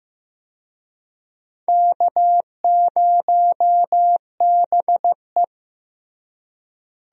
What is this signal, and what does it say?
Morse code, heard as a keyed tone, about 16 wpm, 700 Hz. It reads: K0BE
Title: USA Call Signs - 4 Characters - Repeated 3x - 15wpm